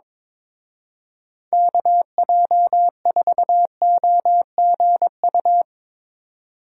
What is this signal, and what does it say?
Morse code, heard as a keyed tone, about 25 wpm, 700 Hz. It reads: KJ4OGU
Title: Speed Racing - USA 2x3 Call Signs 22wpm